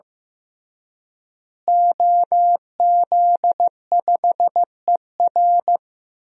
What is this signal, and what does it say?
Morse code, heard as a keyed tone, about 15 wpm, 700 Hz. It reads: OZ5ER
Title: DXCC-100 Call Signs - 5 Characters - Repeated 3x - 15wpm